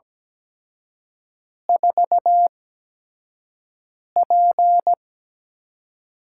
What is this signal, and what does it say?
Morse code, heard as a keyed tone, about 17 wpm, spaced as for 5 wpm, 700 Hz. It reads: 4P
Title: USA Call Signs - 5 Characters - 8x Character Spacing - 17wpm